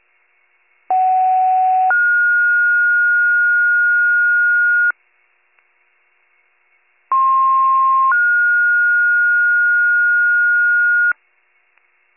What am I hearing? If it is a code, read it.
Two-tone page, heard as a one-second tone, then a three-second tone, about 750 then 1430 Hz, about 1060 then 1430 Hz. It is heard from 0.9 to 11.1 s.